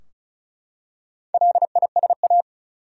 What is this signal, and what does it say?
Morse code, heard as a keyed tone, about 35 wpm, 700 Hz. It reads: LISA
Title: USA Common Names - Max 4 Letters - 35wpm